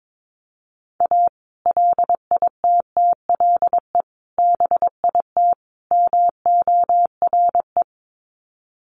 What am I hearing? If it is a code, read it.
Morse code, heard as a keyed tone, about 22 wpm, 700 Hz. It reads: A LITTLE BIT MORE
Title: Sets of 4 Words 22wpm